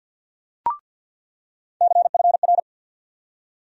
Morse code, heard as a keyed tone, about 47 wpm, 700 Hz. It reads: CFR
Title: USA Call Sign Postfixes - 3 Letters 50wpm